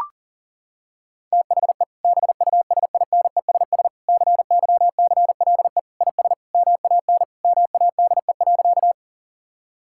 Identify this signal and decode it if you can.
Morse code, heard as a keyed tone, about 40 wpm, 700 Hz. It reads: THE BUSINESS CYCLE IS MAN MADE.
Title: Sentences from Top 1000 Words - Review Part 6 - 40wpm